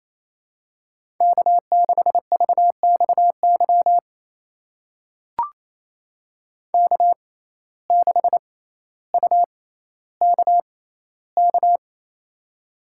Morse code, heard as a keyed tone, about 28 wpm, spaced as for 14 wpm, 700 Hz. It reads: K6VXY K6UKK
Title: USA Call Signs - 5 Characters - 6x Character Spacing - 28wpm